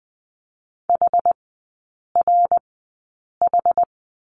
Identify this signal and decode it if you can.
Morse code, heard as a keyed tone, about 20 wpm, spaced as for 8 wpm, 690 Hz. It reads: HRH
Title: 060 Lesson 11e - Letter h - Three Characters 20wpm